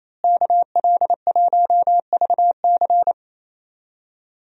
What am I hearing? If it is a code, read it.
Morse code, heard as a keyed tone, about 27 wpm, 700 Hz. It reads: KL1VC
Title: USA Call Signs - 5 Characters - Repeated 2x - 28wpm